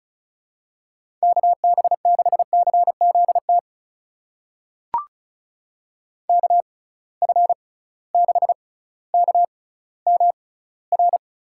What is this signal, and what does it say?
Morse code, heard as a keyed tone, about 36 wpm, spaced as for 18 wpm, 700 Hz. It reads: KB6CZT KF6KMR
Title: USA Call Signs - 6 Characters - 6x Character Spacing - 35wpm